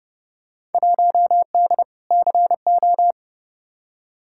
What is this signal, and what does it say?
Morse code, heard as a keyed tone, about 29 wpm, 700 Hz. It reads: 1D CO
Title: Field Day Exchanges - 30wpm